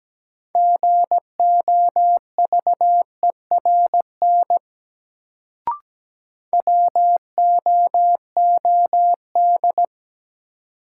Morse code, heard as a keyed tone, about 17 wpm, 700 Hz. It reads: GOVERN WOOD
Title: Sentences from Top 1000 Words - Review Part 2 - New Word Drill - 17wpm